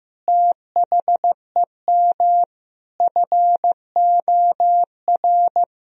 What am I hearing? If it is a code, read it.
Morse code, heard as a keyed tone, about 15 wpm, 700 Hz. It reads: THEM FOR
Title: Sentences from Top 1000 Words - Review Part 1 - 15wpm